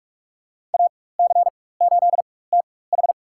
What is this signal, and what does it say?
Morse code, heard as a keyed tone, about 46 wpm, spaced as for 21 wpm, 700 Hz. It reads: AC8TH
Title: USA Call Signs - 4x Character Spacing - 45wpm